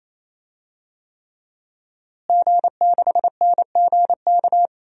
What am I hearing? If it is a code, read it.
Morse code, heard as a keyed tone, about 28 wpm, 700 Hz. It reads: G6NGK
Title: DXCC-100 Call Signs - 5 Characters - Repeated 3x - 28wpm